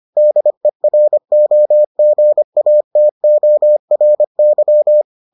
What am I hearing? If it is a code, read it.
Morse code, heard as a keyed tone, about 25 wpm, 600 Hz. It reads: DEROGATORY